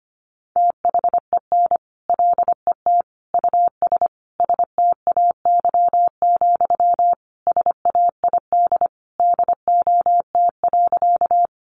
Morse code, heard as a keyed tone, about 25 wpm, 700 Hz. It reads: THEN LET US STAY, SAID DOT.
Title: Sentences from Top 1500 Words 25wpm